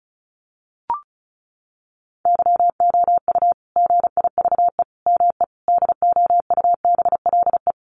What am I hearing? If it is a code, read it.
Morse code, heard as a keyed tone, about 34 wpm, 700 Hz. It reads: YOU GIVE ME DOUBLE
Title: Sentences from Top 1400 Words 35wpm